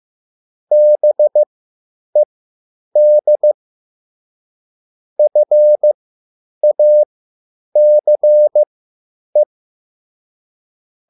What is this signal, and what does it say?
Morse code, heard as a keyed tone, about 15 wpm, spaced as for 9 wpm, 600 Hz. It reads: BED FACE